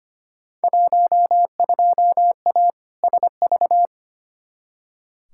Morse code, heard as a keyed tone, about 25 wpm, 700 Hz. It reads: 12A SV
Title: Field Day Exchanges Encore - 25wpm